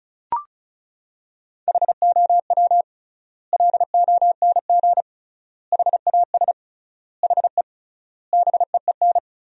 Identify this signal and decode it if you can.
Morse code, heard as a keyed tone, about 34 wpm, 700 Hz. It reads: HOW LONG HAS HE BEEN
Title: 3x Word-Spacing - Sentences from Top 100 Words - 35wpm